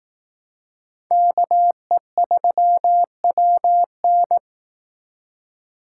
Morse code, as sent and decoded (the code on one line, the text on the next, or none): -.- . ...-- .-- -.
KE3WN